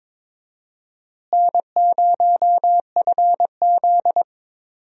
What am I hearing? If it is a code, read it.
Morse code, heard as a keyed tone, about 22 wpm, 700 Hz. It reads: N0FZ
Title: USA Call Signs - 4 Characters - Repeated 2x - 22wpm